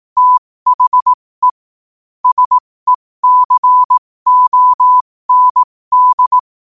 Morse code, heard as a keyed tone, about 18 wpm, spaced as for 15 wpm, 1000 Hz. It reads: THE SECOND
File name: qotdfiles / 15wpm